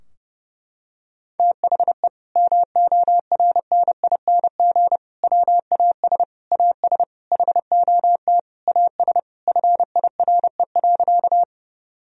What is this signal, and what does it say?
Morse code, heard as a keyed tone, about 30 wpm, 700 Hz. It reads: THE MORNING WAS AS HOT AS FIRE.
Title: Sentences from Top 1000 Words - Review Part 3 - 30wpm